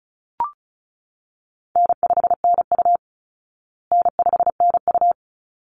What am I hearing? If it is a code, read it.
Morse code, heard as a keyed tone, about 34 wpm, 700 Hz. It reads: N5NU N5NU
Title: Call Signs Repeated 3x - 35wpm